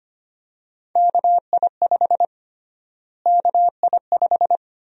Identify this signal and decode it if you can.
Morse code, heard as a keyed tone, about 25 wpm, 700 Hz. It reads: KI5 KI5
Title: USA Call Sign Prefixes - Repeated 2x - 25wpm